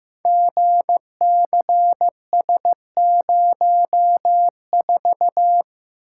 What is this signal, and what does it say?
Morse code, heard as a keyed tone, about 15 wpm, 700 Hz. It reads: GCS04
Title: Code Groups - 5 Letter:Numbers - 15wpm